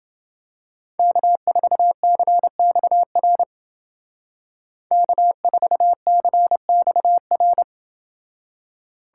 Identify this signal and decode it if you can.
Morse code, heard as a keyed tone, about 29 wpm, 700 Hz. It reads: K4CXR K4CXR